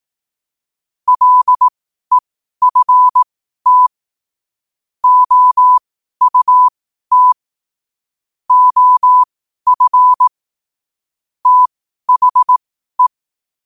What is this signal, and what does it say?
Morse code, heard as a keyed tone, about 18 wpm, spaced as for 12 wpm, 1000 Hz. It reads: LEFT OUT OF THE
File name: qotdfiles / 12wpm